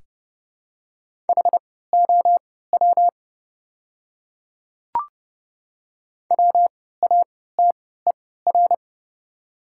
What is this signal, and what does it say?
Morse code, heard as a keyed tone, about 30 wpm, spaced as for 17 wpm, 700 Hz. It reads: HOW WATER
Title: 3x Character Spacing - Top 100 Words - 30wpm